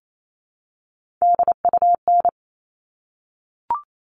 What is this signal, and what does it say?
Morse code, heard as a keyed tone, about 28 wpm, 700 Hz. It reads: DUN
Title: All Cut Numbers - 3-Digit 28wpm